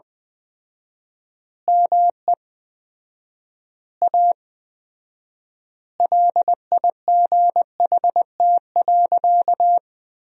Morse code, heard as a keyed tone, about 20 wpm, 700 Hz. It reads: ME A LIGHT.
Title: 4x Word-Spacing - Sentences from Top 200 Words - 20wpm